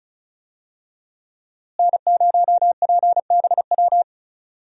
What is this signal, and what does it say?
Morse code, heard as a keyed tone, about 36 wpm, 700 Hz. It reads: N0PBW
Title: USA Call Signs - 5 Characters - Repeated 2x - 35wpm